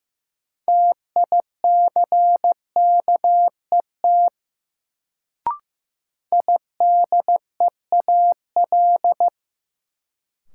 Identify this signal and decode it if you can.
Morse code, heard as a keyed tone, about 15 wpm, 700 Hz. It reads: TICKET IDEAL
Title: Sentences from Top 1400 Words - New Word Drill - 15wpm